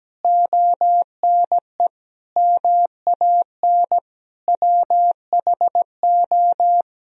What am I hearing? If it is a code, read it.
Morse code, heard as a keyed tone, about 17 wpm, 700 Hz. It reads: ONE MAN WHO
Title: Sentences from Top 1000 Words - Review Part 2 - 17wpm